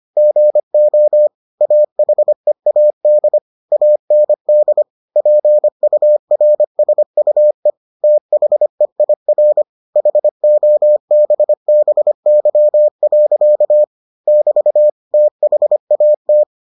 Morse code, heard as a keyed tone, about 25 wpm, 600 Hz. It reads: GO AHEAD AND PURSUE THEIR HOBBY. <BT> THAT